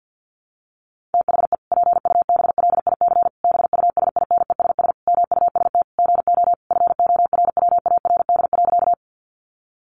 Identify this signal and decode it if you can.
Morse code, heard as a keyed tone, about 50 wpm, 700 Hz. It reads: THE PUBLIC BUSINESS MUST GO FORWARD.
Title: Sentences from Top 1300 Words 50wpm